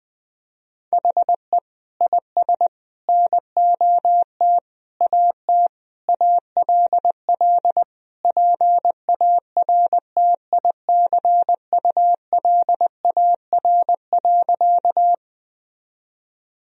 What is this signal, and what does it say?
Morse code, heard as a keyed tone, about 20 wpm, 700 Hz. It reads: HE IS NOT AT ALL PARTICULAR.
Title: Sentences from Top 1000 Words - Review Part 4 - 20wpm